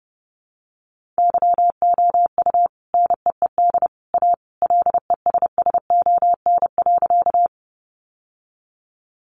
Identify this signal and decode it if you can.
Morse code, heard as a keyed tone, about 30 wpm, 700 Hz. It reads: YOU NEED A LESSON.
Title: Sentences from Top 1200 Words 30wpm